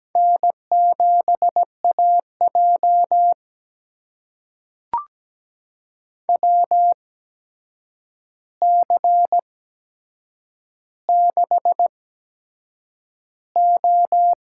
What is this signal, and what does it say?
Morse code, heard as a keyed tone, about 17 wpm, spaced as for 7 wpm, 700 Hz. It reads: N7AJ WC6O
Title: USA Call Signs - 4 Characters - 8x Character Spacing - 17wpm